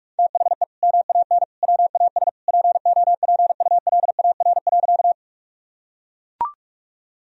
Morse code, heard as a keyed tone, about 43 wpm, 700 Hz. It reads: THE MAN WAS POPULAR.
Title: Sentences from Top 1000 Words - Review Part 3 - 45wpm